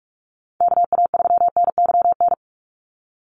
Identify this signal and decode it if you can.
Morse code, heard as a keyed tone, about 44 wpm, 700 Hz. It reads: KA3NYN
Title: USA Call Signs 45wpm